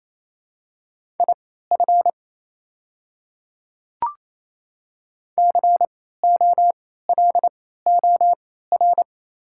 Morse code, heard as a keyed tone, about 28 wpm, spaced as for 16 wpm, 700 Hz. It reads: IF COLOR
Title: 3x Character Spacing - Top 500 Words - 28wpm